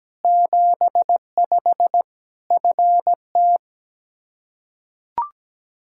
Morse code, heard as a keyed tone, about 17 wpm, 700 Hz. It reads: 75 FT
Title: Practice QSO - Exchange 2 - Info - 17wpm